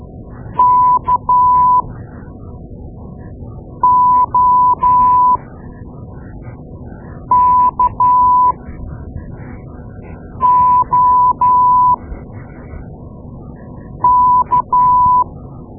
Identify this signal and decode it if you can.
Morse code, heard as a keyed tone, about 13 wpm, spaced as for 4 wpm, 1000 Hz. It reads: KOKOK
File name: Morse Lessons / Morse Lessons / Tape-2A